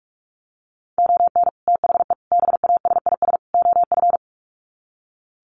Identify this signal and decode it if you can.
Morse code, heard as a keyed tone, about 44 wpm, 700 Hz. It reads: ON THE BASIS OF
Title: Sets of 4 Words 45wpm